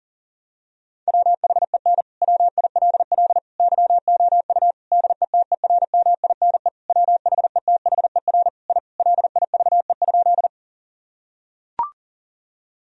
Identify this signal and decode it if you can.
Morse code, heard as a keyed tone, about 38 wpm, 700 Hz. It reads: WHEN WILL YOU DETERMINE WHETHER I LIVE?
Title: Sentences from Top 1000 Words - Review Part 4 - 40wpm